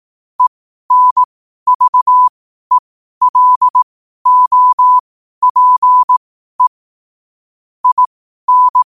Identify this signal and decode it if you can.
Morse code, heard as a keyed tone, about 18 wpm, spaced as for 12 wpm, 1000 Hz. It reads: ENVELOPE IN
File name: qotdfiles / 12wpm